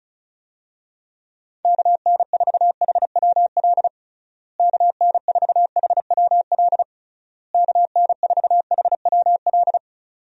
Morse code, heard as a keyed tone, about 34 wpm, 700 Hz. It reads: KN4HWL KN4HWL KN4HWL
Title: USA Call Signs - 6 Characters - Repeated 3x - 35wpm